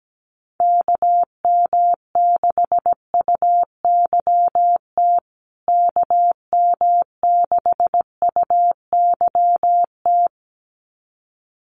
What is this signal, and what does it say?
Morse code, heard as a keyed tone, about 17 wpm, 700 Hz. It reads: KM6UYT KM6UYT
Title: USA Call Signs Repeated 2x 17wpm